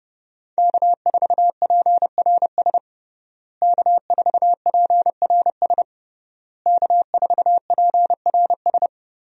Morse code, heard as a keyed tone, about 32 wpm, 700 Hz. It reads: K4PRS K4PRS K4PRS
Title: USA Call Signs - 5 Characters - Repeated 3x - 30wpm